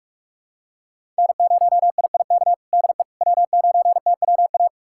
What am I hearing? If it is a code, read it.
Morse code, heard as a keyed tone, about 46 wpm, 700 Hz. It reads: N0IIK DE W9TWA
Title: USA Callsign DE Callsign - 5 Characters - 45wpm